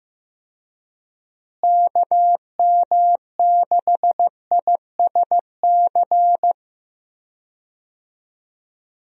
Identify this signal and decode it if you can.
Morse code, heard as a keyed tone, about 15 wpm, 700 Hz. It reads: KM6ISC
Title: USA Call Signs - 6 Characters - Repeated 3x - 15wpm